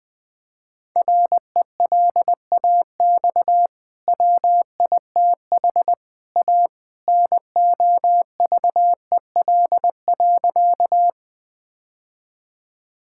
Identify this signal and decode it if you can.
Morse code, heard as a keyed tone, about 20 wpm, 700 Hz. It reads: RELAX WITH A NOVEL.
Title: Easy Sentences with Hints - 20wpm